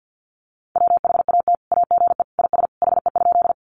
Morse code, heard as a keyed tone, about 50 wpm, 700 Hz. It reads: WHAT AGE IS HE?
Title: Sentences from Top 500 Words 50wpm